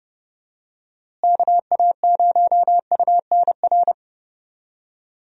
Morse code, heard as a keyed tone, about 32 wpm, 700 Hz. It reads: KA0UNR
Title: USA Call Signs - 6 Characters - Repeated 2x - 30wpm